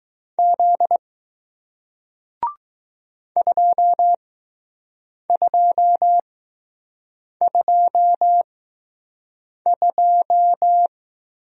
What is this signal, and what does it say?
Morse code, heard as a keyed tone, about 20 wpm, spaced as for 6 wpm, 700 Hz. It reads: Z 2222